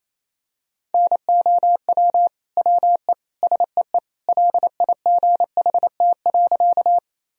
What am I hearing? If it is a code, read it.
Morse code, heard as a keyed tone, about 28 wpm, 700 Hz. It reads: NOW WE SEE LIGHT.
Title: Sentences from Top 1000 Words - Review Part 1 - 28wpm